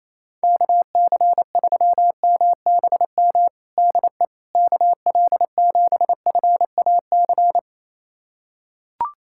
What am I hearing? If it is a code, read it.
Morse code, heard as a keyed tone, about 28 wpm, 700 Hz. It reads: KC3MBM DE KL7FAC
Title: USA Callsign DE Callsign 28wpm